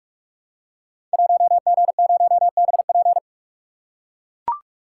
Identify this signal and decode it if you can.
Morse code, heard as a keyed tone, about 44 wpm, 700 Hz. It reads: 1G0BP
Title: Code Groups - 5 Letter:Numbers - 45wpm